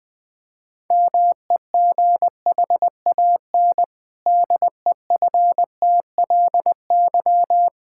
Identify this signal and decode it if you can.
Morse code, heard as a keyed tone, about 21 wpm, 700 Hz. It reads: MEGHAN DEFTLY
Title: Pangram Challenge v2 - 20wpm